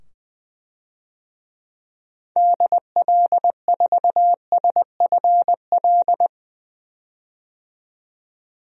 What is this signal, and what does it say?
Morse code, heard as a keyed tone, about 20 wpm, 700 Hz. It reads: DL4SFL
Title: DXCC-100 Call Signs - 6 Characters - 20wpm